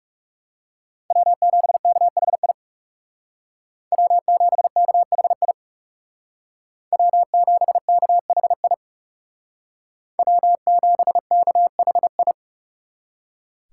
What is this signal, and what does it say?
Morse code, heard as a keyed tone, about 35 wpm, 700 Hz. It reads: W7KHI W7KHI W7KHI W7KHI
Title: Speed Racing - USA 1x3 Call Signs 30wpm